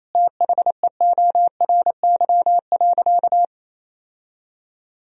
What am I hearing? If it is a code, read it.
Morse code, heard as a keyed tone, about 28 wpm, 700 Hz. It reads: THEORY.
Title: Sentences from Top 1000 Words - Review Part 4 - 28wpm